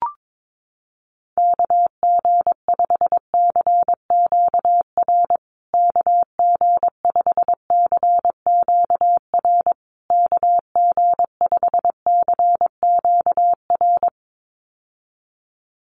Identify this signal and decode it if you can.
Morse code, heard as a keyed tone, about 22 wpm, 700 Hz. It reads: KG5CQR KG5CQR KG5CQR
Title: USA Call Signs Repeated 3x 22wpm